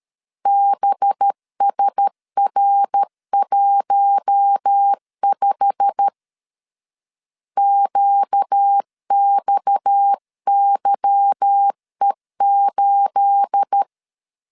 Morse code, heard as a keyed tone, about 13 wpm, 800 Hz. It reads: BSR15 QXYE8
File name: CW / DET / 1 9 1500C S A 12wpm 3800